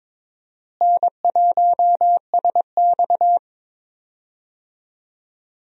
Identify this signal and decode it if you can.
Morse code, heard as a keyed tone, about 23 wpm, 700 Hz. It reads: N1SX
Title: USA Call Signs - 4 Characters - Repeated 3x - 22wpm